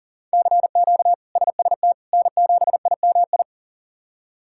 Morse code, heard as a keyed tone, about 40 wpm, 690 Hz. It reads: CQ SST N7IMI
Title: K1USN SST QSO Practice - 40wpm